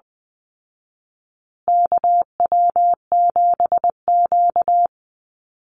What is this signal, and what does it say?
Morse code, heard as a keyed tone, about 20 wpm, 700 Hz. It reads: KW7Q
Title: Call Signs Repeated 3x - 20wpm